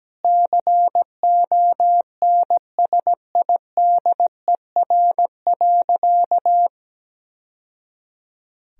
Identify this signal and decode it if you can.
Morse code, heard as a keyed tone, about 17 wpm, 700 Hz. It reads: CONSIDER.
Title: Sentences from Top 1000 Words - Review Part 4 - 17wpm